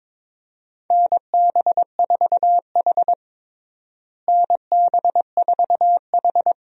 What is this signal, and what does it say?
Morse code, heard as a keyed tone, about 22 wpm, 700 Hz. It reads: NB4H NB4H
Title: USA Call Signs - 4 Characters - Repeated 3x - 22wpm